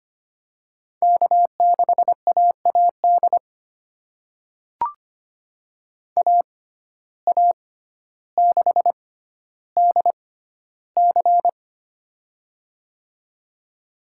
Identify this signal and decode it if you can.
Morse code, heard as a keyed tone, about 25 wpm, spaced as for 13 wpm, 700 Hz. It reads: K6AAD AA6DC
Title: USA Call Signs - 6x Character Spacing - 25wpm